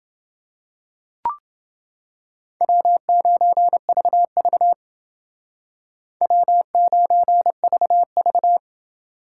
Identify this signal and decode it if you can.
Morse code, heard as a keyed tone, about 29 wpm, 700 Hz. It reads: W9VV W9VV